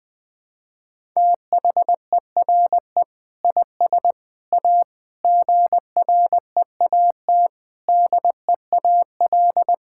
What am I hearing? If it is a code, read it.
Morse code, heard as a keyed tone, about 20 wpm, 700 Hz. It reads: THERE IS A GREAT DEAL
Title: Sets of 6 Words - 20wpm